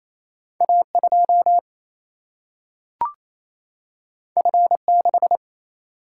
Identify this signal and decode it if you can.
Morse code, heard as a keyed tone, about 28 wpm, 700 Hz. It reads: A2 F6
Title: Single-Letter Single-Number 28wpm